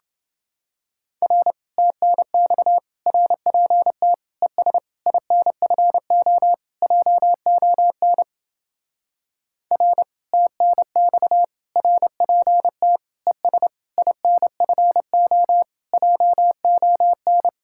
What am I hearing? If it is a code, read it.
Morse code, heard as a keyed tone, about 28 wpm, 700 Hz. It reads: R TNX RPT ES INFO JON R TNX RPT ES INFO JON